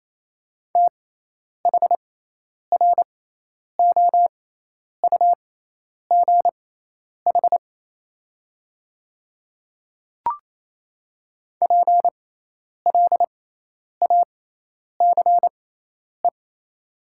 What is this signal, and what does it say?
Morse code, heard as a keyed tone, about 28 wpm, spaced as for 10 wpm, 700 Hz. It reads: THROUGH PLACE
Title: 6x Character Spacing - Top 200 Words - 28wpm